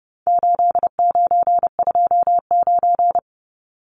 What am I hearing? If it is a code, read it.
Morse code, heard as a keyed tone, about 30 wpm, 700 Hz. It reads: 8929
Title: Numbers - 4-Digit 30wpm